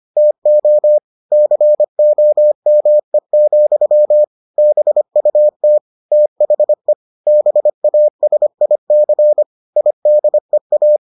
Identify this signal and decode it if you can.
Morse code, heard as a keyed tone, about 25 wpm, 600 Hz. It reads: TO COME, BUT THE BASIC IDEA